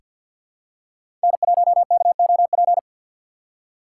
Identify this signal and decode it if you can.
Morse code, heard as a keyed tone, about 49 wpm, 700 Hz. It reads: N1KOP
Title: USA Call Signs - 4x Character Spacing - 50wpm